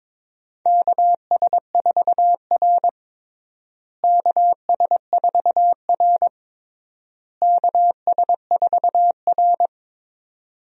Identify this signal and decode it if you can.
Morse code, heard as a keyed tone, about 22 wpm, 700 Hz. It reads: KS4R KS4R KS4R
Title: USA Call Signs - 4 Characters - Repeated 3x - 22wpm